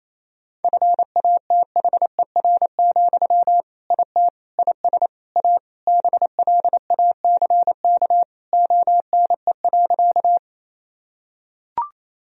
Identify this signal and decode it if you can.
Morse code, heard as a keyed tone, about 28 wpm, 700 Hz. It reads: FATHER, IT IS A BLACK ONE.
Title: Sentences from Top 1000 Words - Review Part 2 - 28wpm